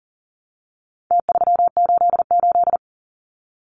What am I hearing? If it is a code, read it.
Morse code, heard as a keyed tone, about 40 wpm, 700 Hz. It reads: T388
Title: Numbers - Contest Style - 4-Digit 40wpm